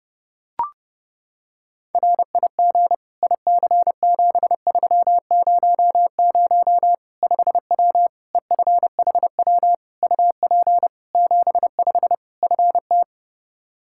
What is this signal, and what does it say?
Morse code, heard as a keyed tone, about 30 wpm, 700 Hz. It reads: RIG IC7300 5W EFHW UP 75 FT
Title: Practice QSO - Exchange 2 - Info - 30wpm